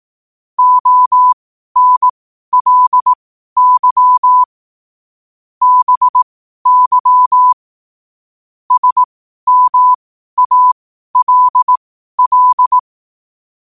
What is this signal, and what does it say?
Morse code, heard as a keyed tone, about 18 wpm, spaced as for 12 wpm, 1000 Hz. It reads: ONLY BY SMALL
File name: qotdfiles / 12wpm